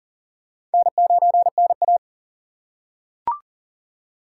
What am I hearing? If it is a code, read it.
Morse code, heard as a keyed tone, about 40 wpm, 700 Hz. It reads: N9NA
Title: USA Call Signs - 4 Characters - 4x Character Spacing - 40wpm